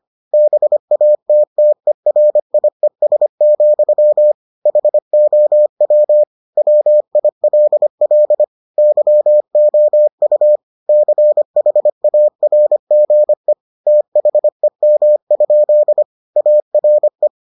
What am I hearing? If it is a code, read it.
Morse code, heard as a keyed tone, about 25 wpm, 600 Hz. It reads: BATTERIES, HOW WILL YOU CHARGE THEM? ARE